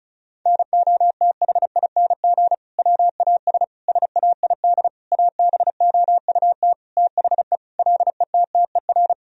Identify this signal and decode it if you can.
Morse code, heard as a keyed tone, about 36 wpm, 700 Hz. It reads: NOTHING WAS SAID ABOUT THE LETTER
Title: Sentences from Top 1000 Words - Review Part 2 - 35wpm